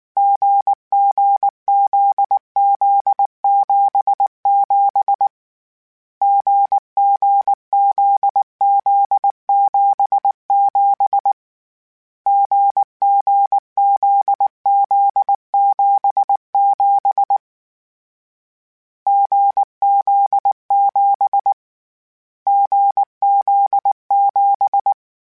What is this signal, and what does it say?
Morse code, heard as a keyed tone, about 19 wpm, 800 Hz. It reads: GGZZ77 GGZZ77 GGZZ77 GZ7 GZ7
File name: Difficult Combos / Difficult Combos-mp3s / GZ GZ7 @18